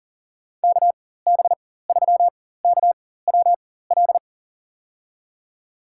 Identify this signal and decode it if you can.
Morse code, heard as a keyed tone, about 40 wpm, spaced as for 19 wpm, 700 Hz. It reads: KB3KWL
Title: USA Call Signs - 4x Character Spacing - 40wpm